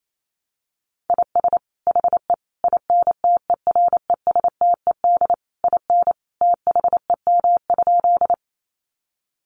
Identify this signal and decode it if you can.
Morse code, heard as a keyed tone, about 28 wpm, 700 Hz. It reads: IS HE INTERESTED IN THEM?
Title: Sentences from Top 900 Words 28wpm